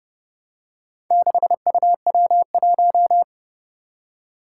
Morse code, heard as a keyed tone, about 30 wpm, 700 Hz. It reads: 6UW1
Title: Four Challenging Letter-Number Combinations 30wpm